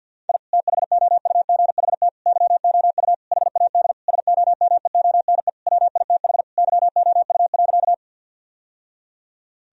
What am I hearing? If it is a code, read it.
Morse code, heard as a keyed tone, about 47 wpm, 700 Hz. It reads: I THOUGHT YOU HAD SOMEONE WITH YOU.
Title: Sentences from Top 1000 Words - Review Part 4 - 50wpm